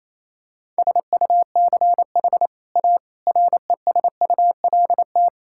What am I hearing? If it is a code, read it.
Morse code, heard as a keyed tone, about 28 wpm, 700 Hz. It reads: SUCH A RESULT
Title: Sentences from Top 1000 Words - Review Part 5 - 28wpm